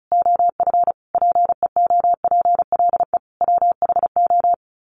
Morse code, heard as a keyed tone, about 34 wpm, 700 Hz. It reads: OF PEOPLE WHO
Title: Sets of 5 Words 35wpm